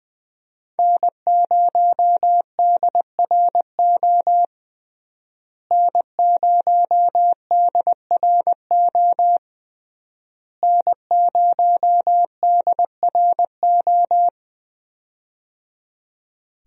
Morse code, heard as a keyed tone, about 21 wpm, 700 Hz. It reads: N0DRO N0DRO N0DRO
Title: USA Call Signs - 5 Characters - Repeated 3x - 20wpm